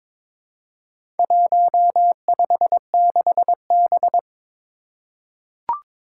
Morse code, heard as a keyed tone, about 22 wpm, 700 Hz. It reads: 156B
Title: Four Challenging Letter-Number Combinations 22wpm